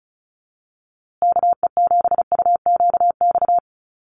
Morse code, heard as a keyed tone, about 34 wpm, 700 Hz. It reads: KE7UQX
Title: USA Call Signs Repeated 2x 35wpm